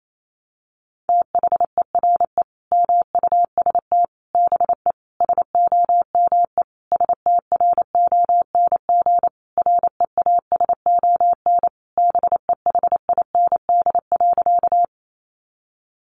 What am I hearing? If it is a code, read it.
Morse code, heard as a keyed tone, about 28 wpm, 700 Hz. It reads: THERE MUST BE SOME STRONG REASON BEHIND.
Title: Sentences from Top 700 Words 28wpm